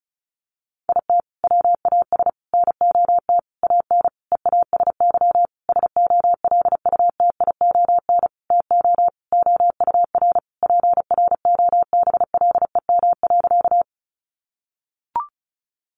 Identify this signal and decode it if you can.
Morse code, heard as a keyed tone, about 34 wpm, 700 Hz. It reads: IT WAS NOT AN EASY SOLUTION TO OUR PROBLEM.
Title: Sentences from Top 1300 Words 35wpm